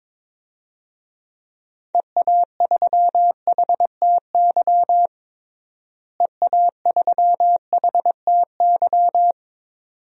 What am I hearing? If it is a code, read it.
Morse code, heard as a keyed tone, about 22 wpm, 700 Hz. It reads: EA3HTY EA3HTY
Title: DXCC-100 Call Signs - 6 Characters - Repeated 2x - 22wpm